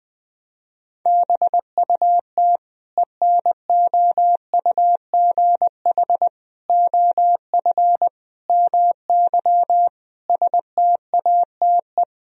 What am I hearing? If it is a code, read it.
Morse code, heard as a keyed tone, about 21 wpm, 700 Hz. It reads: BUT ENOUGH OF MY STATE
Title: Sentences from Top 1000 Words - Review Part 2 - 20wpm